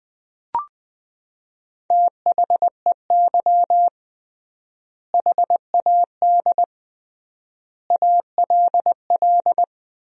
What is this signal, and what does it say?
Morse code, heard as a keyed tone, about 20 wpm, 700 Hz. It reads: THEY HAD ALL
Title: 3x Word-Spacing - Sentences from Top 200 Words - 20wpm